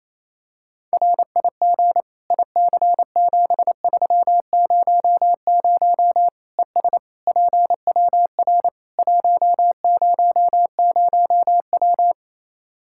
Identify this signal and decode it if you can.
Morse code, heard as a keyed tone, about 27 wpm, 700 Hz. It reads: RIG IC7300 ES PWR 100W
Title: Practice QSO - Exchange 2 - Info - 28wpm